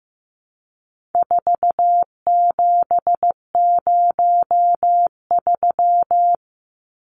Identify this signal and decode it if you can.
Morse code, heard as a keyed tone, about 15 wpm, 700 Hz. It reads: 4703
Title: Numbers - 4-Digit 15wpm